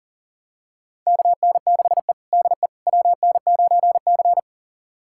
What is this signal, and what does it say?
Morse code, heard as a keyed tone, about 37 wpm, 700 Hz. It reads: KN6E DE WN9C
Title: USA Callsign DE Callsign - 4 Characters - 40wpm